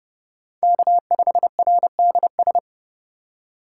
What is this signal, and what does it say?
Morse code, heard as a keyed tone, about 37 wpm, 700 Hz. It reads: K5RDS
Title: Speed Racing - USA 1x3 Call Signs 30wpm